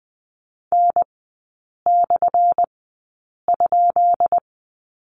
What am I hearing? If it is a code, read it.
Morse code, heard as a keyed tone, about 20 wpm, spaced as for 8 wpm, 700 Hz. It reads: N/?
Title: 178 Lesson 29e - Slash - Three Characters 20wpm